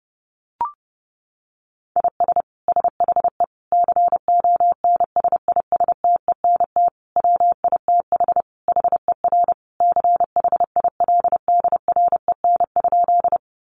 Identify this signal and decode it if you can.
Morse code, heard as a keyed tone, about 30 wpm, 700 Hz. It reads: IS SHE CONSISTENT WITH HER CHILDREN?
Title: Sentences from Top 1500 Words 30wpm